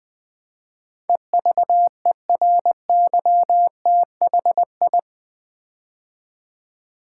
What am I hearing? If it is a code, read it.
Morse code, heard as a keyed tone, about 20 wpm, 700 Hz. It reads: EVERYTHI
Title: Build A Word - 4 Letters At A Time - 10-Letter Words - 20wpm